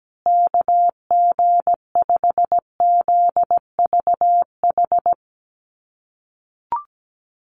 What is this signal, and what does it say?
Morse code, heard as a keyed tone, about 17 wpm, 700 Hz. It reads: KG5ZVH
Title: USA Call Signs Repeated 2x 17wpm